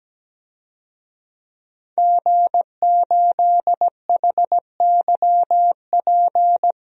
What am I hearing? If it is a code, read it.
Morse code, heard as a keyed tone, about 17 wpm, 700 Hz. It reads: G8HYP
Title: DXCC-100 Call Signs - 5 Characters - 17wpm